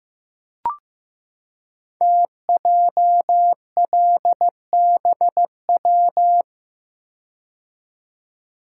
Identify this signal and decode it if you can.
Morse code, heard as a keyed tone, about 15 wpm, 700 Hz. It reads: TJLBW
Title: Code Groups - 5 Letter:Numbers - 15wpm